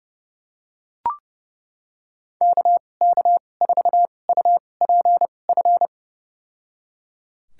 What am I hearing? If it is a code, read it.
Morse code, heard as a keyed tone, about 30 wpm, spaced as for 22 wpm, 700 Hz. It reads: KK4UPF
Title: USA Call Signs - 2x Character Spacing - 30wpm